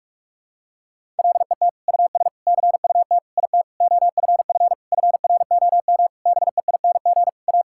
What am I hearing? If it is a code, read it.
Morse code, heard as a keyed tone, about 46 wpm, 700 Hz. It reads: LET US CUT IT OFF FROM BEING A
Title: Sentences from Top 1000 Words - Review Part 6 - 45wpm